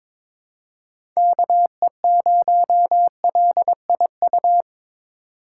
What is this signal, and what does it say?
Morse code, heard as a keyed tone, about 23 wpm, 700 Hz. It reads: KE0LIU
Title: USA Call Signs - 6 Characters - 6x Character Spacing - 22wpm